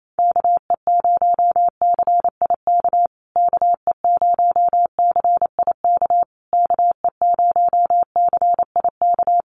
Morse code, heard as a keyed tone, about 28 wpm, 700 Hz. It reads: KE0CIK KE0CIK KE0CIK
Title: USA Call Signs Repeated 3x 28wpm